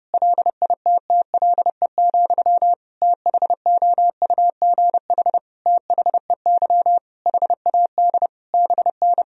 Morse code, heard as a keyed tone, about 32 wpm, 700 Hz. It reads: LITTLE, THOUGH THEY HAD BN
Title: Sentences from Top 400 Words and Abbreviations 30wpm